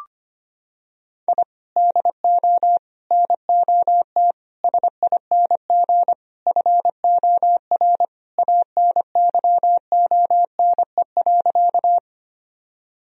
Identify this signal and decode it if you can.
Morse code, heard as a keyed tone, about 25 wpm, 700 Hz. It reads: I DO NOT SING FOR ANYONE.
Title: Sentences from Top 1000 Words - Review Part 2 - 25wpm